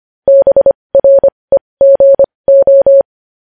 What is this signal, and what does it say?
Morse code, heard as a keyed tone, about 25 wpm, spaced as for 20 wpm, 550 Hz. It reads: BREGO
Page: News Headlines in Morse Code at 20 WPM